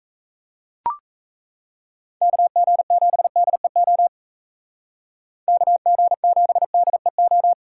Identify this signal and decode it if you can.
Morse code, heard as a keyed tone, about 35 wpm, 700 Hz. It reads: KG7DEO KG7DEO
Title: Speed Racing - USA 2x3 Call Signs 28wpm